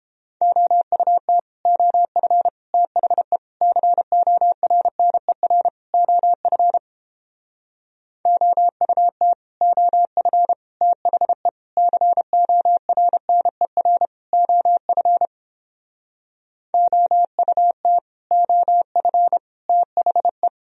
Morse code, heard as a keyed tone, about 31 wpm, 700 Hz. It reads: OUT OF THE CORNER OF OUT OF THE CORNER OF OUT OF THE